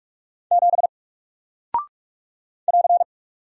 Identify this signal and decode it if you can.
Morse code, heard as a keyed tone, about 44 wpm, 700 Hz. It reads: 7 <AR>
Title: Single Character (Letter:Number:Punctuation:Prosign) 45wpm